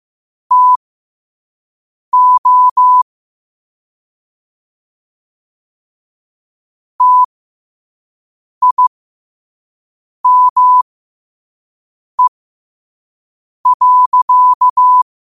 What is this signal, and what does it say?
Morse code, heard as a keyed tone, about 15 wpm, spaced as for 5 wpm, 1000 Hz. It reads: TO TIME.